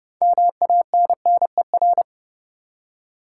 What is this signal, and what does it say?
Morse code, heard as a keyed tone, about 30 wpm, 700 Hz. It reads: MANNER
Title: Six Letter Words - 30wpm